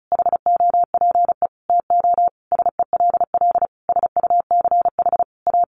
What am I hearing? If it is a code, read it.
Morse code, heard as a keyed tone, about 34 wpm, 700 Hz. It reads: HOPE TO SELL SUCH A
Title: Sentences from Top 1100 Words 35wpm